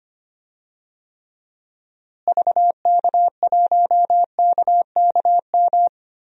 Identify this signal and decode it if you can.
Morse code, heard as a keyed tone, about 25 wpm, 700 Hz. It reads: VK1KKM
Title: DXCC-100 Call Signs - 6 Characters - Repeated 3x - 25wpm